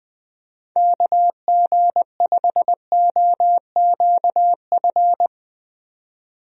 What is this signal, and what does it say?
Morse code, heard as a keyed tone, about 20 wpm, 700 Hz. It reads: KG5OQF
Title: USA Call Signs - 6 Characters - 20wpm